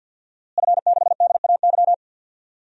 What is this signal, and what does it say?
Morse code, heard as a keyed tone, about 50 wpm, 700 Hz. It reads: F6DAY
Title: DXCC-100 Call Signs - 5 Characters - Repeated 3x - 50wpm